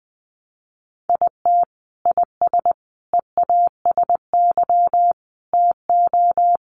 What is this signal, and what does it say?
Morse code, heard as a keyed tone, about 19 wpm, 700 Hz. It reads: IT IS EASY TO
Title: Sentences from Top 1300 Words 20wpm